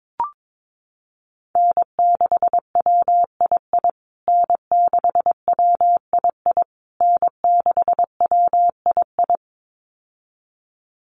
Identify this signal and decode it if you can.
Morse code, heard as a keyed tone, about 22 wpm, 700 Hz. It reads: N6WII N6WII N6WII
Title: USA Call Signs Repeated 3x 22wpm